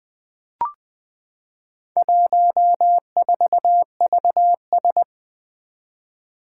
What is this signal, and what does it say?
Morse code, heard as a keyed tone, about 20 wpm, 700 Hz. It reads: 14VS
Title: Four Challenging Letter-Number Combinations 20wpm